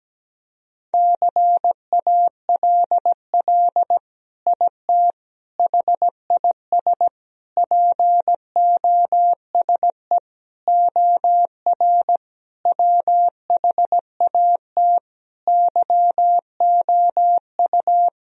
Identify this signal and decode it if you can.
Morse code, heard as a keyed tone, about 17 wpm, 700 Hz. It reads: CALL IT HIS POSE OR WHAT YOU
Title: Sentences from Top 1000 Words - Review Part 2 - 17wpm